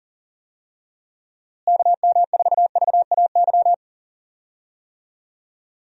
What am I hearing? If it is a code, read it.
Morse code, heard as a keyed tone, about 40 wpm, 700 Hz. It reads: KM4VAY
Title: USA Call Signs - 6 Characters - Repeated 2x - 40wpm